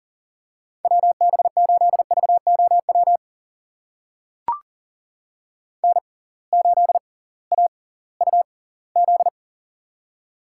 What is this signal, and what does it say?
Morse code, heard as a keyed tone, about 40 wpm, spaced as for 22 wpm, 700 Hz. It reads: WB8VOW N8AUZ
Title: USA Call Signs - 6x Character Spacing - 40wpm